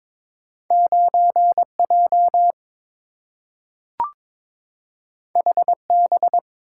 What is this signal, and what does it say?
Morse code, heard as a keyed tone, about 22 wpm, 700 Hz. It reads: 9J HB
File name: practice / 22wpm / Two Letter-Number Combinations - 22wpm